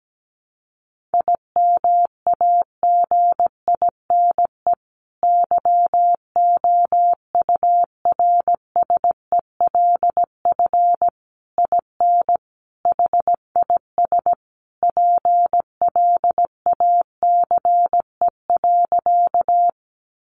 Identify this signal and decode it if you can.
Morse code, heard as a keyed tone, about 17 wpm, 700 Hz. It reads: IMAGINE YOURSELF IN HIS PLACE.
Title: Sentences from Top 1400 Words 17wpm